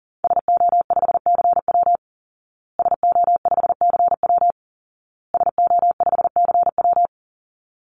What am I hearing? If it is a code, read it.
Morse code, heard as a keyed tone, about 40 wpm, 700 Hz. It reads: SO5CW SO5CW SO5CW
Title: Call Signs Repeated 2x - 40wpm